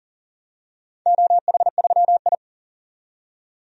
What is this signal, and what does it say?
Morse code, heard as a keyed tone, about 40 wpm, 700 Hz. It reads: OH3I
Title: DXCC-100 Call Signs - 4 Characters - 40wpm